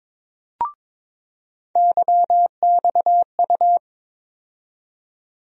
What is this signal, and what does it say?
Morse code, heard as a keyed tone, about 22 wpm, 700 Hz. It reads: YXU
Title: USA Call Sign Postfixes - 3 Letters 22wpm